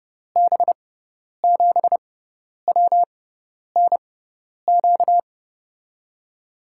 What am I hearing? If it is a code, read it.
Morse code, heard as a keyed tone, about 29 wpm, spaced as for 10 wpm, 700 Hz. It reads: B7WNQ
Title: USA Call Signs - 6x Character Spacing - 30wpm